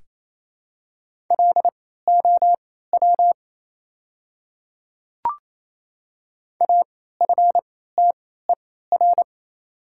Morse code, heard as a keyed tone, about 27 wpm, spaced as for 16 wpm, 700 Hz. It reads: LOW AFTER
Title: 3x Character Spacing - Top 200 Words - 28wpm